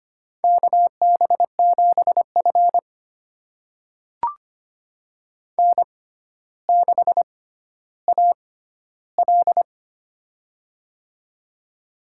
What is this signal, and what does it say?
Morse code, heard as a keyed tone, about 25 wpm, spaced as for 13 wpm, 700 Hz. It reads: KB7F N6AL
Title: USA Call Signs - 4 Characters - 6x Character Spacing - 25wpm